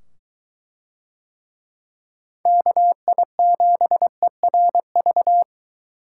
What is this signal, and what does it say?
Morse code, heard as a keyed tone, about 23 wpm, 700 Hz. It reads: KI7ERV